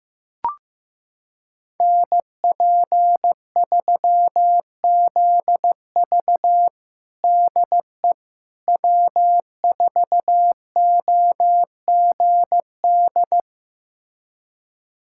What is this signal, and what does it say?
Morse code, heard as a keyed tone, about 15 wpm, 700 Hz. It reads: NP3ZV DE W4OGD
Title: USA Callsign DE Callsign - 5 Characters - 15wpm